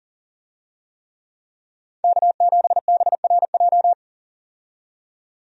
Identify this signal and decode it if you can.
Morse code, heard as a keyed tone, about 40 wpm, 700 Hz. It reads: K7BRJ
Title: USA Call Signs - 5 Characters - 40wpm